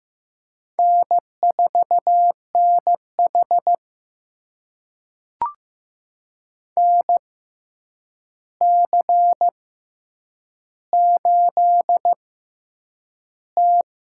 Morse code, heard as a keyed tone, about 15 wpm, spaced as for 8 wpm, 700 Hz. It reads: N4NH NC8T
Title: USA Call Signs - 4 Characters - 6x Character Spacing - 15wpm